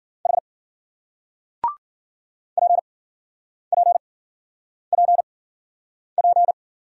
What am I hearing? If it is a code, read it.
Morse code, heard as a keyed tone, about 52 wpm, spaced as for 9 wpm, 700 Hz. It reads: H PPPP